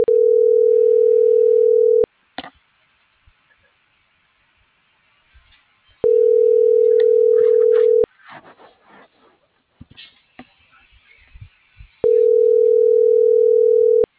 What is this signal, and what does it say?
Ringback tone (North America), 440 + 480 Hz.
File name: CENTRAL BLN / out ericka.canales